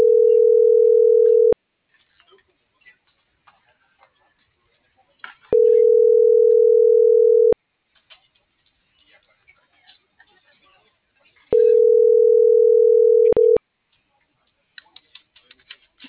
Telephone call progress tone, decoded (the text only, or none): ringback tone